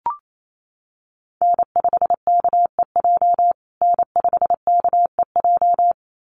Morse code, heard as a keyed tone, about 28 wpm, 700 Hz. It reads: N5KEJ N5KEJ
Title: USA Call Signs Repeated 2x 28wpm